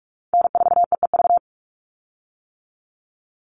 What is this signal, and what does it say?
Morse code, heard as a keyed tone, about 44 wpm, 700 Hz. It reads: N4EEV
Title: Call Signs - 45wpm